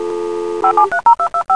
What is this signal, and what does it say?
DTMF, heard as telephone touch tones, tones of about 90 ms. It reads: 573225